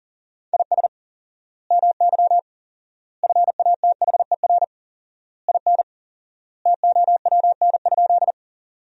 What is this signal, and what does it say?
Morse code, heard as a keyed tone, about 38 wpm, 700 Hz. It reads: IS MY FATHER IN TOWN?
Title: 4x Word-Spacing - Sentences from Top 500 Words - 40wpm